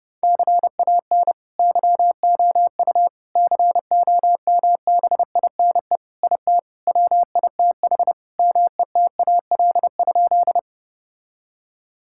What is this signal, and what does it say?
Morse code, heard as a keyed tone, about 30 wpm, 700 Hz. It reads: CAN YOU COMBINE IT WITH METAL?
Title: Sentences from Top 1600 Words 30wpm